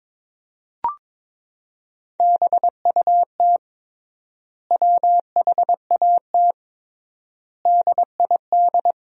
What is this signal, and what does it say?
Morse code, heard as a keyed tone, about 22 wpm, 700 Hz. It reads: BUT WHAT DID
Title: 3x Word-Spacing - Sentences from Top 200 Words - 22wpm